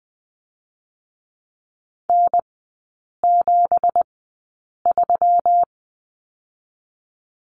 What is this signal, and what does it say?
Morse code, heard as a keyed tone, about 20 wpm, spaced as for 8 wpm, 690 Hz. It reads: N73
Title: 212 Lesson 34e - Number 8 - Three Characters 20wpm